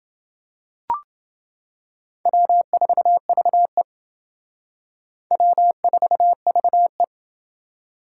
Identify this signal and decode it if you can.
Morse code, heard as a keyed tone, about 28 wpm, 700 Hz. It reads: W4VE W4VE